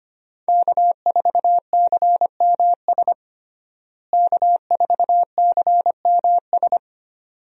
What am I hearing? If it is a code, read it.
Morse code, heard as a keyed tone, about 25 wpm, 700 Hz. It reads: K4CMS K4CMS
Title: USA Call Signs - 5 Characters - Repeated 2x - 25wpm